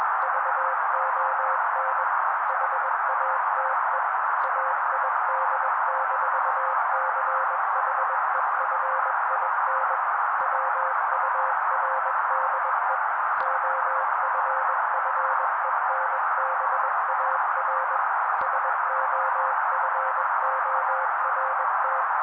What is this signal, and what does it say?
Morse code, heard as a keyed tone, about 21 wpm, 530 Hz. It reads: VON HATE AID-CHEFIN WURDE OFFENBAR SOFORT